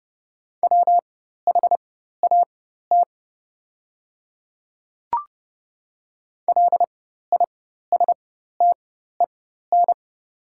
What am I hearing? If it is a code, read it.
Morse code, heard as a keyed tone, about 30 wpm, spaced as for 14 wpm, 700 Hz. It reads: WHAT LISTEN
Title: 4x Character Spacing - Top 500 Words - 30wpm